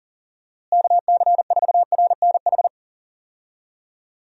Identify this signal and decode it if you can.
Morse code, heard as a keyed tone, about 37 wpm, 700 Hz. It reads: KC4RNH
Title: USA Call Signs - 6 Characters - Repeated 3x - 40wpm